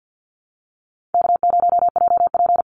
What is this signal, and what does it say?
Morse code, heard as a keyed tone, about 50 wpm, 700 Hz. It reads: K0JP
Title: Easy Call Signs Repeated 3x - 50wpm